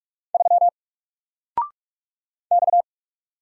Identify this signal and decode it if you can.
Morse code, heard as a keyed tone, about 46 wpm, 700 Hz. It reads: 3 X
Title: Single Letter-Number 45wpm